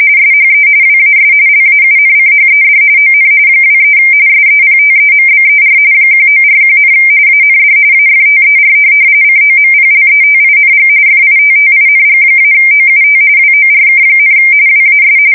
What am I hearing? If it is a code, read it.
RTTY, about 45.45 baud. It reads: SHE WAS LAUNCHED ON 23 SEPTEMBER 1941, AND AFTER COMPLETING FITTING-OUT WORK, WAS
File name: fsk170 / USS-Massachusetts